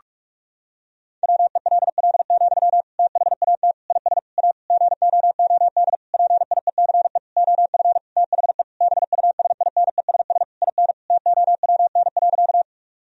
Morse code, heard as a keyed tone, about 45 wpm, 700 Hz. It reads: WELL, THAT IS A GOOD PIECE OF THE BUSINESS IN TOWN.
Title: Sentences from Top 1000 Words - Review Part 2 - 45wpm